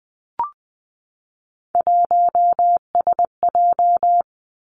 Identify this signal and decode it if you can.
Morse code, heard as a keyed tone, about 19 wpm, 700 Hz. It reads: 1SJ
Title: Three Challenging Letter-Number Combinations 20wpm